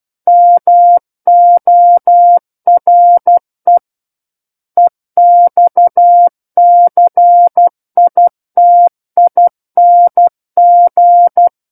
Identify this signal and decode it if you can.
Morse code, heard as a keyed tone, about 12 wpm, 700 Hz. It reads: MORE EXCITING